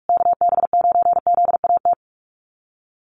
Morse code, heard as a keyed tone, about 44 wpm, 700 Hz. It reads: KB9ZAT
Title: USA Call Signs Repeated 3x 45wpm